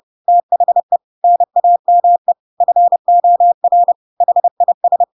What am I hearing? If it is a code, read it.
Morse code, heard as a keyed tone, about 30 wpm, 700 Hz. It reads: THE NAME FOR HIS
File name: Pinocchio / Pinocchio ch3 30 30 ik1hge